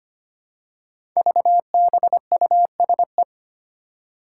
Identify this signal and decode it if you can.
Morse code, heard as a keyed tone, about 25 wpm, 700 Hz. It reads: VBUSE
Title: Five Challenging Letter-Number Combinations 25wpm